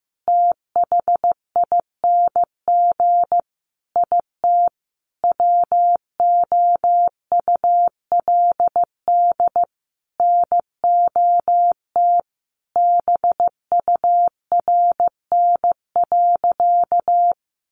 Morse code, heard as a keyed tone, about 15 wpm, 700 Hz. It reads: THING IT WOULD NOT BURN.
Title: Sentences from Top 1300 Words 15wpm